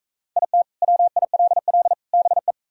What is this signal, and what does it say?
Morse code, heard as a keyed tone, about 40 wpm, 700 Hz. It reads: IT WILL BE